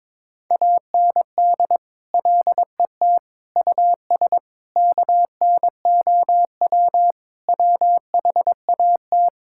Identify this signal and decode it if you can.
Morse code, heard as a keyed tone, about 22 wpm, 700 Hz. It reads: AND LET US KNOW WHAT
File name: practice / 22wpm / Sets of 7 Words - 22wpm